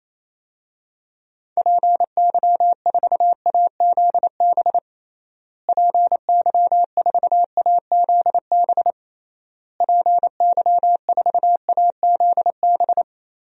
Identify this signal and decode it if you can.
Morse code, heard as a keyed tone, about 27 wpm, 700 Hz. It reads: PY4AZB PY4AZB PY4AZB
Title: DXCC-100 Call Signs - 6 Characters - Repeated 3x - 28wpm